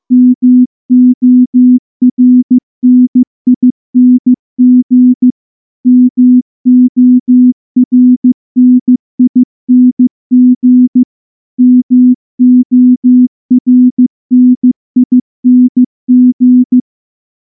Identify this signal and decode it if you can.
Morse code, heard as a keyed tone, about 14 wpm, 250 Hz. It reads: MORNING MORNING MORNING
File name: morse-training-15WPM